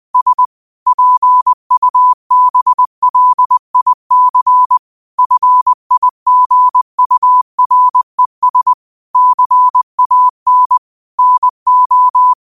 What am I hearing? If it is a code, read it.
Morse code, heard as a keyed tone, about 20 wpm, 1000 Hz. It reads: S PUBLIC FIGURES CAN NO
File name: qotdfiles / 20wpm